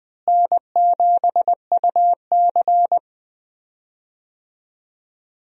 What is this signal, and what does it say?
Morse code, heard as a keyed tone, about 21 wpm, 700 Hz. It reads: N7UC
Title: USA Call Signs - 4 Characters - Repeated 2x - 20wpm